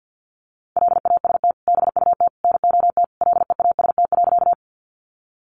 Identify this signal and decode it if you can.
Morse code, heard as a keyed tone, about 48 wpm, 700 Hz. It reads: LAST BUT NOT LEAST.
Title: Sentences from Top 700 Words 50wpm